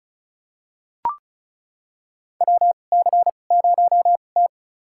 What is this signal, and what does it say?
Morse code, heard as a keyed tone, about 36 wpm, spaced as for 25 wpm, 700 Hz. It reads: WC0T
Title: USA Call Signs - 4 Characters - 2x Character Spacing - 35wpm